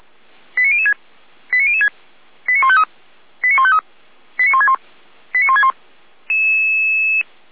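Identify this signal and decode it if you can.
Five-tone selective call (ZVEI), heard as quick runs of five tones, tones of about 70 ms. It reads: 89006 89006 89153 89153 89161 89161
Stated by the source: ZVEI five-tone call received by SDR, archived by feuerwehr-orlamuende.org (fire brigade Orlamünde).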